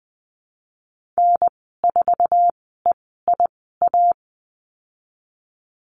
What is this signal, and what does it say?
Morse code, heard as a keyed tone, about 20 wpm, spaced as for 14 wpm, 700 Hz. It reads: N4EIA
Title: 048 Lesson 9g - Number 4 - Call Signs 20wpm